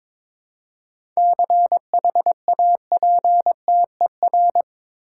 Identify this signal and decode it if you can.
Morse code, heard as a keyed tone, about 22 wpm, 700 Hz. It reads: CHAPTER
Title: Seven Letter Words - 22wpm